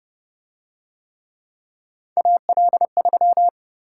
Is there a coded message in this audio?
Yes: Morse code from 2.2 to 3.5 s, heard as a keyed tone, about 29 wpm, 700 Hz. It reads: AL3